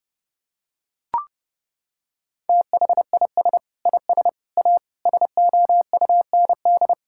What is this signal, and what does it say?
Morse code, heard as a keyed tone, about 32 wpm, 700 Hz. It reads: THIS IS A SOUND
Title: Sentences from Top 1000 Words - Review Part 6 - 30wpm